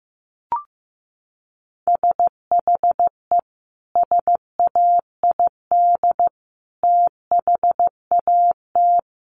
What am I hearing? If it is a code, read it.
Morse code, heard as a keyed tone, about 15 wpm, 700 Hz. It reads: SHE SAID THAT
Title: Sentences from Top 1200 Words 15wpm